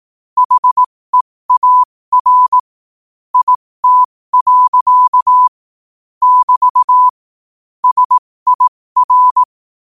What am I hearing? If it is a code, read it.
Morse code, heard as a keyed tone, about 18 wpm, spaced as for 15 wpm, 1000 Hz. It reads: HEAR IT. <BT> SIR
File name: qotdfiles / 15wpm